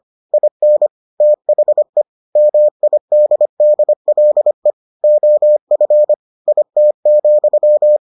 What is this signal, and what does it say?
Morse code, heard as a keyed tone, about 25 wpm, 600 Hz. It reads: IN THE MIDDLE OF IT,